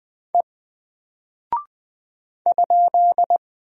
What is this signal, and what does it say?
Morse code, heard as a keyed tone, about 20 wpm, 700 Hz. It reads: E ?
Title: Single Character (Letter:Number:Punctuation:Prosign) 20wpm